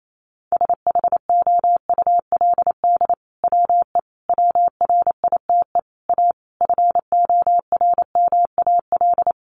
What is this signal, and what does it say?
Morse code, heard as a keyed tone, about 28 wpm, 700 Hz. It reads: SHOULD WE WRITE A FORMAL
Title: Sentences from Top 1300 Words 28wpm